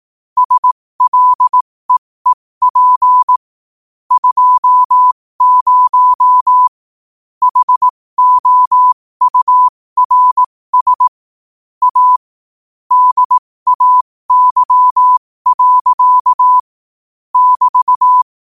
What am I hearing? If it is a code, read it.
Morse code, heard as a keyed tone, about 18 wpm, spaced as for 15 wpm, 1000 Hz. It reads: SLEEP 20 HOURS A DAY. <BT>
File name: qotdfiles / 15wpm